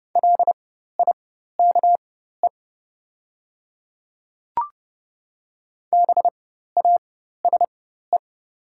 Morse code, heard as a keyed tone, about 29 wpm, spaced as for 14 wpm, 700 Hz. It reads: LIKE BASE
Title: 4x Character Spacing - Top 300 Words - 30wpm